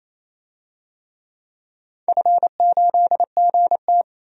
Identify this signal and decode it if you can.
Morse code, heard as a keyed tone, about 28 wpm, 700 Hz. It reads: F8GT
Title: DXCC-100 Call Signs - 4 Characters - 28wpm